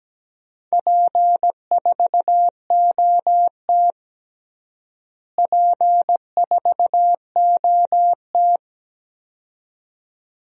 Morse code, heard as a keyed tone, about 17 wpm, 700 Hz. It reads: P4OT P4OT
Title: DXCC-100 Call Signs - 4 Characters - Repeated 2x - 17wpm